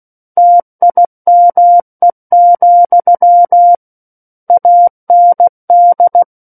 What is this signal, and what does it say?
Morse code, heard as a keyed tone, about 16 wpm, 700 Hz. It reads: TIME, AND
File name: Book One / 07-Chapter 7 - 16 WPM